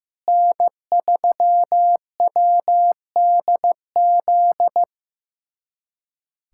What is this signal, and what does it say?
Morse code, heard as a keyed tone, about 15 wpm, 700 Hz. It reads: N3WDZ
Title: USA Callsign DE Callsign 15wpm